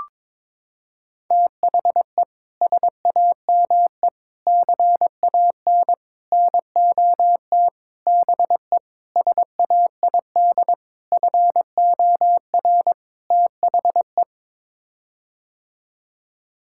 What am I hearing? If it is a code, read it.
Morse code, heard as a keyed tone, about 22 wpm, 700 Hz. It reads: THE SAME CAN NOT BE SAID FOR THE
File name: practice / 22wpm / Sets of 8 Words - 22wpm